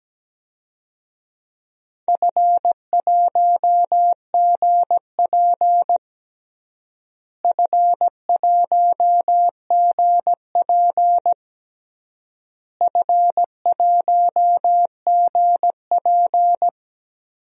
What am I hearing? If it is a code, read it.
Morse code, heard as a keyed tone, about 17 wpm, 700 Hz. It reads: F1GP F1GP F1GP
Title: DXCC-100 Call Signs - 4 Characters - Repeated 3x - 17wpm